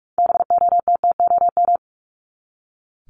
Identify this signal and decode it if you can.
Morse code, heard as a keyed tone, about 44 wpm, 700 Hz. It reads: BOTTOM
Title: Top 751-1000 Words 45wpm